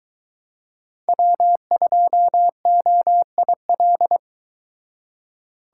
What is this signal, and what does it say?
Morse code, heard as a keyed tone, about 22 wpm, 700 Hz. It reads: W2OIL